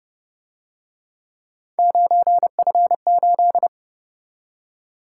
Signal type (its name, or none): Morse code